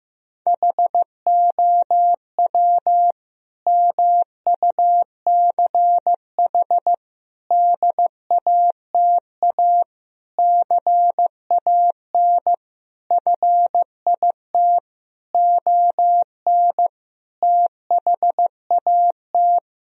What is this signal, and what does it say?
Morse code, heard as a keyed tone, about 16 wpm, 700 Hz. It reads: HOW MUCH DATA CAN FIT ON THAT
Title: Sentences from Top 1000 Words - Review Part 4 - 15wpm